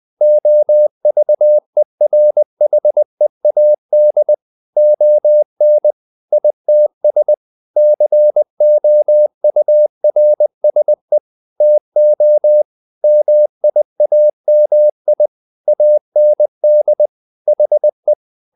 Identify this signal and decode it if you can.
Morse code, heard as a keyed tone, about 20 wpm, 600 Hz. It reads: OVERHEAD ON ITS COURSE TO MIAMI AND HE
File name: the old man and the sea part 2 20wpm